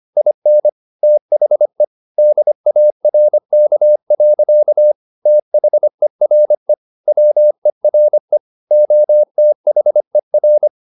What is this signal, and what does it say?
Morse code, heard as a keyed tone, about 25 wpm, 600 Hz. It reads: IN THE DARK. THERE WERE OTHER
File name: the old man and the sea part 1 25wpm